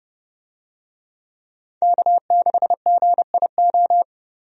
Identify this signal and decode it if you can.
Morse code, heard as a keyed tone, about 31 wpm, 700 Hz. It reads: K6GIO
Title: USA Call Signs - 5 Characters - 30wpm